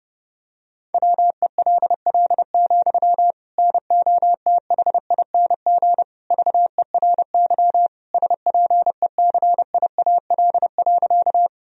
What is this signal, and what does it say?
Morse code, heard as a keyed tone, about 32 wpm, 700 Hz. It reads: WELL, NOTHING VERY SPECIAL.
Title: Sentences from Top 1000 Words - Review Part 3 - 30wpm